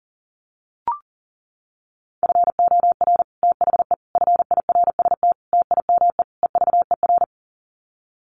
Morse code, heard as a keyed tone, about 40 wpm, 700 Hz. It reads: FOR THE FIRST TIME EVER
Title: Sets of 5 Words Encore2 40wpm